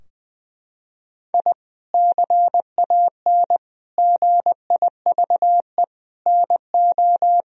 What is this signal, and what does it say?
Morse code, heard as a keyed tone, about 20 wpm, 700 Hz. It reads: I CAN GIVE NO
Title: Sentences from Top 1000 Words - Review Part 3 - 20wpm